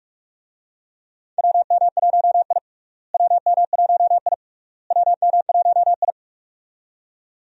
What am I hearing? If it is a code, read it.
Morse code, heard as a keyed tone, about 48 wpm, 700 Hz. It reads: WM1I WM1I WM1I
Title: USA Call Signs - 4 Characters - Repeated 3x - 45wpm